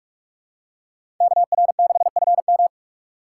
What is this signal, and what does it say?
Morse code, heard as a keyed tone, about 46 wpm, 700 Hz. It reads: KR6FM
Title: USA Call Signs - 5 Characters - 4x Character Spacing - 45wpm